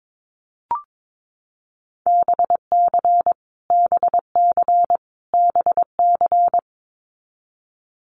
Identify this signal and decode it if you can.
Morse code, heard as a keyed tone, about 22 wpm, 700 Hz. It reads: BC BC BC
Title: Canadian Provinces and Territories - Repeated 3x - 22wpm